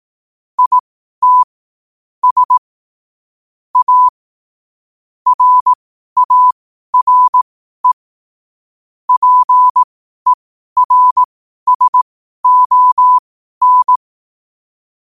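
Morse code, heard as a keyed tone, about 18 wpm, spaced as for 12 wpm, 1000 Hz. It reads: IT S A RARE PERSON
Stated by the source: Morse code at 12 wpm.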